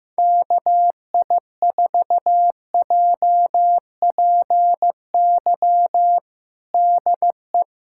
Morse code, heard as a keyed tone, about 15 wpm, 700 Hz. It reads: KI4JPY DE
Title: USA Callsign DE Callsign 15wpm